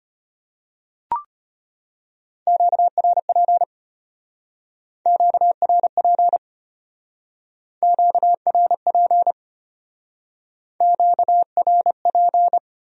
Morse code, heard as a keyed tone, about 33 wpm, 700 Hz. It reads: QRP QRP QRP QRP